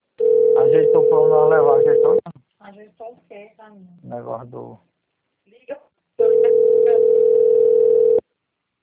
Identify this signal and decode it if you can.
Ringback tone (North America), 440 + 480 Hz, with speech.
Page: Itatiba